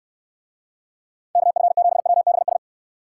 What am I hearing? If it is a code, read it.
Morse code, heard as a keyed tone, about 69 wpm, 700 Hz. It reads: KF7WBS